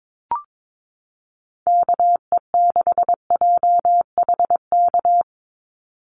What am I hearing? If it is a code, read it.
Morse code, heard as a keyed tone, about 22 wpm, 700 Hz. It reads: KE6JHK
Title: USA Call Signs 22wpm